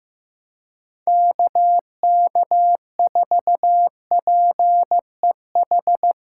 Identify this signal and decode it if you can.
Morse code, heard as a keyed tone, about 15 wpm, 700 Hz. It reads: KK4PEH
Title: USA Callsign DE Callsign - 6 Characters - 15wpm